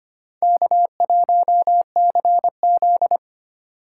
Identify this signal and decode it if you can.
Morse code, heard as a keyed tone, about 25 wpm, 700 Hz. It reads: K1CZ
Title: USA Call Signs - 4 Characters - 3x Character Spacing - 25wpm